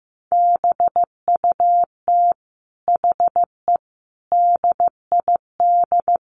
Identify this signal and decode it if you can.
Morse code, heard as a keyed tone, about 15 wpm, 700 Hz. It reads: BUT HE DID
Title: Sentences from Top 1300 Words 15wpm